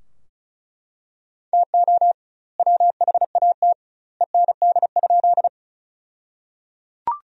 Morse code, heard as a keyed tone, about 34 wpm, 700 Hz. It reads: TO WHAT END?
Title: 2x Word-Spacing - Sentences from Top 200 Words - 35wpm